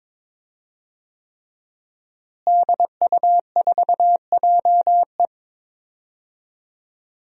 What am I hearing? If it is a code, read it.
Morse code, heard as a keyed tone, about 22 wpm, 700 Hz. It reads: DU4JE
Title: DXCC-100 Call Signs - 5 Characters - Repeated 3x - 22wpm